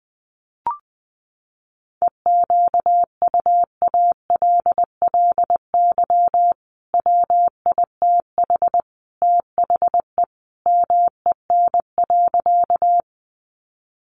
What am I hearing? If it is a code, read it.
Morse code, heard as a keyed tone, about 19 wpm, 700 Hz. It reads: EQUALLY WITH THE MEN.
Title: Sentences from Top 1400 Words 20wpm